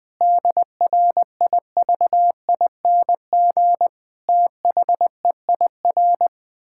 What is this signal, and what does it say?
Morse code, heard as a keyed tone, about 20 wpm, 700 Hz. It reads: DRIVING THEIR
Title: Pangram Challenge v2 - 20wpm